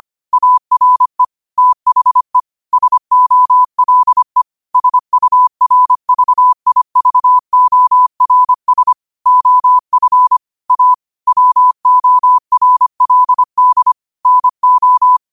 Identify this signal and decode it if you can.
Morse code, heard as a keyed tone, about 25 wpm, 1000 Hz. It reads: ARE THE SOLE SURVIVORS OF A WORLD NO